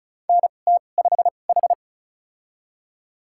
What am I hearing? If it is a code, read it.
Morse code, heard as a keyed tone, about 34 wpm, spaced as for 25 wpm, 700 Hz. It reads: NT5H
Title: USA Call Signs - 2x Character Spacing - 35wpm